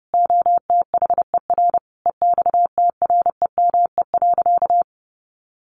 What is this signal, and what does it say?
Morse code, heard as a keyed tone, about 30 wpm, 700 Hz. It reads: OTHER EXTREME.
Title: Sentences from Top 1300 Words 30wpm